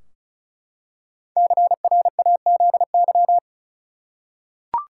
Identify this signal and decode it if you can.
Morse code, heard as a keyed tone, about 36 wpm, 700 Hz. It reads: CRAZY
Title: Five Letter Words - 35wpm